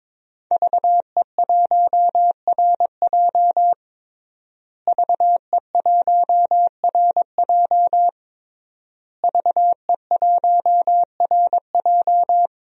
Morse code, heard as a keyed tone, about 22 wpm, 700 Hz. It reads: VE1RJ VE1RJ VE1RJ
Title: DXCC-100 Call Signs - 5 Characters - Repeated 3x - 22wpm